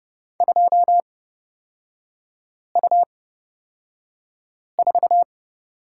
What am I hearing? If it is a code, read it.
Morse code, heard as a keyed tone, about 29 wpm, spaced as for 5 wpm, 690 Hz. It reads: 2U4
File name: Lesson 38e - Letter q - Three Characters 30wpm